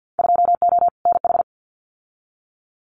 Morse code, heard as a keyed tone, about 50 wpm, 700 Hz. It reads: 2O NH
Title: Winter Field Day Exchanges 50wpm